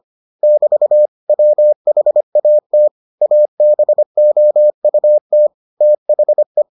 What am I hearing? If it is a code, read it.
Morse code, heard as a keyed tone, about 25 wpm, 600 Hz. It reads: <BT> WHAT ABOUT THE